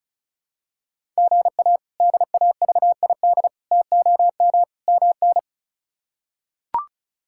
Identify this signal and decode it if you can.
Morse code, heard as a keyed tone, about 36 wpm, 700 Hz. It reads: GA DAVID TOM MN
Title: K1USN SST QSO Practice - 35wpm